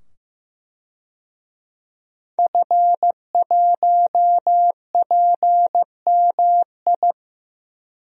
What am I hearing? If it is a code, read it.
Morse code, heard as a keyed tone, about 15 wpm, 700 Hz. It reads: F1PMI
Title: DXCC-100 Call Signs - 5 Characters - 15wpm